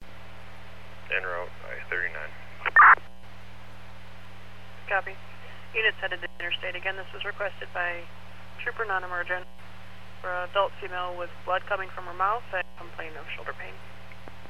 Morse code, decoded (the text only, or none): none